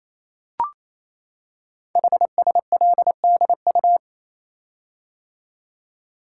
Morse code, heard as a keyed tone, about 28 wpm, 700 Hz. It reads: HSLDU
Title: Code Groups - 5 Letter:Numbers - 28wpm